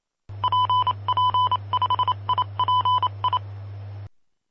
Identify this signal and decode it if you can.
Morse code, heard as a keyed tone, about 27 wpm, 1000 Hz. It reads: PP5IPI
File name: pp5bar / aulas de telegrafia / Aula-02